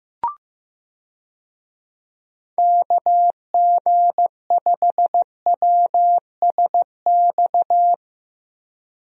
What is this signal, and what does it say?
Morse code, heard as a keyed tone, about 15 wpm, 700 Hz. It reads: KG5WSX
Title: USA Call Signs - 6 Characters - Repeated 3x - 15wpm